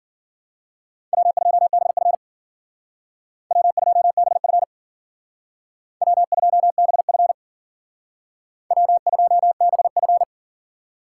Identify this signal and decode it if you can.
Morse code, heard as a keyed tone, about 51 wpm, 700 Hz. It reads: W2BF W2BF W2BF W2BF